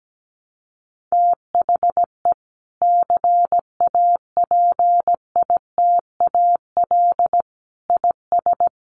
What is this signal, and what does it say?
Morse code, heard as a keyed tone, about 17 wpm, 700 Hz. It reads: THE CAPITAL IS
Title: Sentences from Top 1000 Words 17wpm